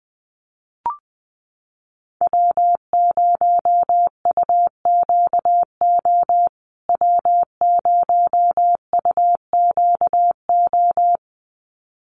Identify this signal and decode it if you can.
Morse code, heard as a keyed tone, about 20 wpm, 700 Hz. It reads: W0UQO W0UQO
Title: USA Call Signs Repeated 2x 20wpm